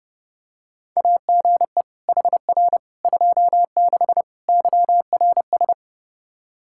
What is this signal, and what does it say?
Morse code, heard as a keyed tone, about 30 wpm, 700 Hz. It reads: AGE HR 26 YRS
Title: Practice QSO - Exchange 3 - More Info - 30wpm